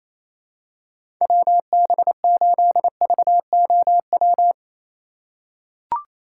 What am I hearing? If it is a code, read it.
Morse code, heard as a keyed tone, about 28 wpm, 700 Hz. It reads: WB8VOW
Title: USA Call Signs - 6 Characters - 4x Character Spacing - 28wpm